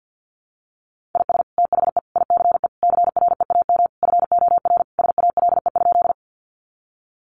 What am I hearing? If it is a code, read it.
Morse code, heard as a keyed tone, about 50 wpm, 700 Hz. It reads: IS THE ICE CREAM FOR SALE?
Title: Sentences from Top 1500 Words 50wpm